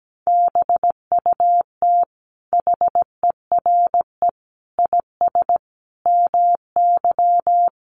Morse code, heard as a keyed tone, about 17 wpm, 700 Hz. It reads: BUT HERE IS MY
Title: Sentences from Top 1500 Words 17wpm